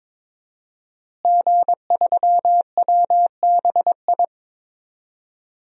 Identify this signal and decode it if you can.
Morse code, heard as a keyed tone, about 22 wpm, 700 Hz. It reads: G3WBI
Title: DXCC-100 Call Signs - 5 Characters - Repeated 3x - 22wpm